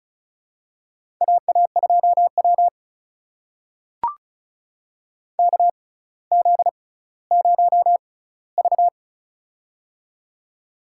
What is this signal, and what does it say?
Morse code, heard as a keyed tone, about 36 wpm, spaced as for 18 wpm, 700 Hz. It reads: AA2W KZ0V
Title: USA Call Signs - 4 Characters - 6x Character Spacing - 35wpm